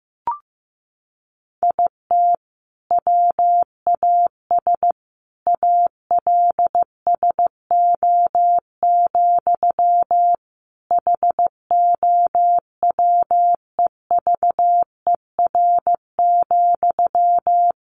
Morse code, heard as a keyed tone, about 15 wpm, 700 Hz. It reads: IT WAS ALSO, HOWEVER,
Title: Sentences from Top 1400 Words 15wpm